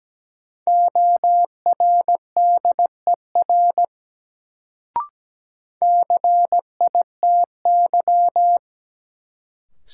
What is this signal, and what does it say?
Morse code, heard as a keyed tone, about 17 wpm, 700 Hz. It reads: ORDER CITY
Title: Sets of 2 Words - New Word Drill - 17wpm